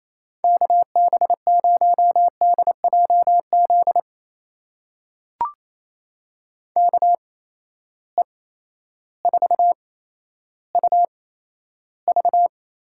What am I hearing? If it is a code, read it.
Morse code, heard as a keyed tone, about 28 wpm, spaced as for 13 wpm, 700 Hz. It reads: KB0DJZ KE4UV
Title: USA Call Signs - 6 Characters - 8x Character Spacing - 28wpm